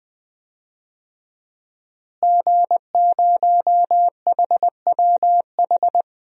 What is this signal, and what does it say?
Morse code, heard as a keyed tone, about 20 wpm, 700 Hz. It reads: G0HWH
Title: DXCC-100 Call Signs - 5 Characters - Repeated 3x - 20wpm